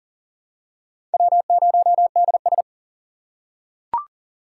Morse code, heard as a keyed tone, about 40 wpm, 700 Hz. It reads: W0DS
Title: USA Call Signs - 4 Characters - 3x Character Spacing - 40wpm